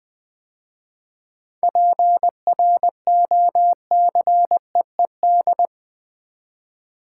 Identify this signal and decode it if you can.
Morse code, heard as a keyed tone, about 20 wpm, 700 Hz. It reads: PROCEED
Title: 7 Letter Words - First 3 Letters Given - 20wpm